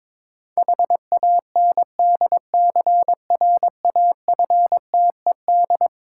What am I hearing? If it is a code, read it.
Morse code, heard as a keyed tone, about 22 wpm, 690 Hz. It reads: HANDCRAFTED
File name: Difficult Words with Hints - 22wpm